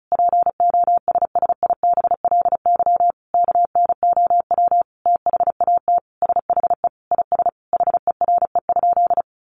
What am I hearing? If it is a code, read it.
Morse code, heard as a keyed tone, about 34 wpm, 700 Hz. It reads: POSSIBLY KNOW THAT SHE IS HERE?
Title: Sentences from Top 1300 Words 35wpm